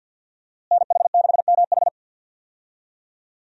Morse code, heard as a keyed tone, about 52 wpm, 700 Hz. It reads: NH6MH
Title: USA Call Signs - 5 Characters - Repeated 2x - 50wpm